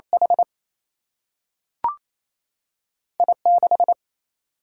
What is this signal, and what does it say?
Morse code, heard as a keyed tone, about 28 wpm, 700 Hz. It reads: H I 6
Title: Single-Letter Single-Number 28wpm